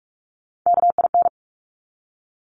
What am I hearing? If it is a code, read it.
Morse code, heard as a keyed tone, about 46 wpm, 700 Hz. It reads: KIN
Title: Top Three Letter Combinations in Words 45wpm